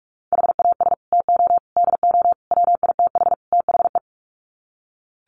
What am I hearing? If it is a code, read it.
Morse code, heard as a keyed tone, about 44 wpm, 700 Hz. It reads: HAS TO DO WITH THE
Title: Sets of 5 Words 45wpm